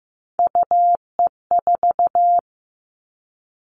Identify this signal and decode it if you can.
Morse code, heard as a keyed tone, about 15 wpm, 700 Hz. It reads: UE4
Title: Three Challenging Letter-Number Combinations 15wpm